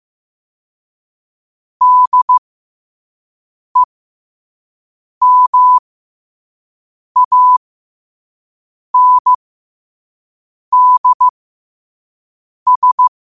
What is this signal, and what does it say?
Morse code, heard as a keyed tone, about 15 wpm, spaced as for 5 wpm, 1000 Hz. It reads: DEMANDS
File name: qotdfiles / 5wpm